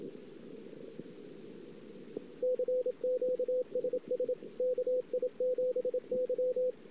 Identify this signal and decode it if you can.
Morse code, heard as a keyed tone, about 27 wpm, 500 Hz, with radio noise. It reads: CQSS KI7Y
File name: ss-cw